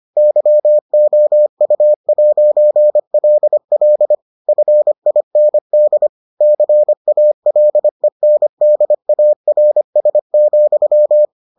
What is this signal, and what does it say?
Morse code, heard as a keyed tone, about 25 wpm, 600 Hz. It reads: YOU'LL FIND CALENDARS,